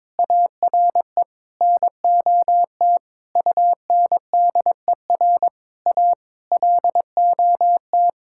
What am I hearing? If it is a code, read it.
Morse code, heard as a keyed tone, about 22 wpm, 700 Hz. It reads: ARE NOT UNDER A LOT
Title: Sentences from Top 1000 Words - Review Part 6 - 22wpm